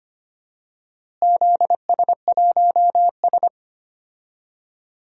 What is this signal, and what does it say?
Morse code, heard as a keyed tone, about 25 wpm, 700 Hz. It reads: ZS1S
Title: DXCC-100 Call Signs - 4 Characters - Repeated 2x - 25wpm